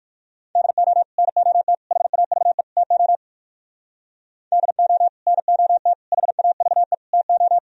Morse code, heard as a keyed tone, about 58 wpm, 700 Hz. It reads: DO NOT HAVE TO DO NOT HAVE TO